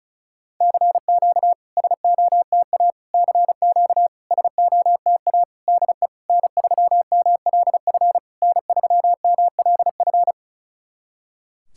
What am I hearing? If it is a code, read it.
Morse code, heard as a keyed tone, about 36 wpm, 700 Hz. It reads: CQ SOTA CQ SOTA DE N3MLF N3MLF
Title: SOTA QSO Practice 35wpm